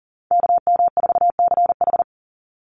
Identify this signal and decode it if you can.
Morse code, heard as a keyed tone, about 40 wpm, 700 Hz. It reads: KM4CH
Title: Call Signs Repeated 2x - 40wpm